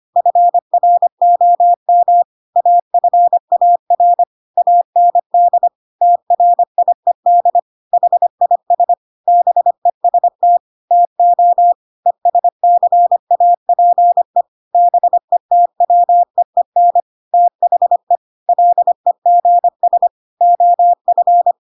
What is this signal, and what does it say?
Morse code, heard as a keyed tone, about 25 wpm, 700 Hz. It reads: FROM AFAR AND TRIED HIS BEST TO ESCAPE BETWEEN THE LEGS OF
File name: Pinocchio / Pinocchio ch3 25 25 ik1hge